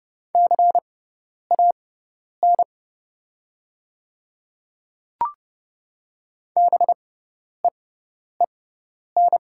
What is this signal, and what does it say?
Morse code, heard as a keyed tone, about 30 wpm, spaced as for 10 wpm, 700 Hz. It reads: CAN BEEN
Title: 6x Character Spacing - Top 100 Words - 30wpm